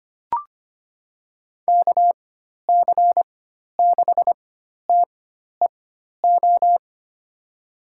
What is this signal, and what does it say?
Morse code, heard as a keyed tone, about 25 wpm, spaced as for 12 wpm, 700 Hz. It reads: KC6TEO
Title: USA Call Signs - 6 Characters - 4x Character Spacing - 25wpm